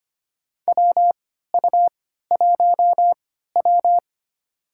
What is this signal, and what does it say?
Morse code, heard as a keyed tone, about 25 wpm, spaced as for 14 wpm, 700 Hz. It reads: WU1W
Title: USA Call Signs - 4 Characters - 3x Character Spacing - 25wpm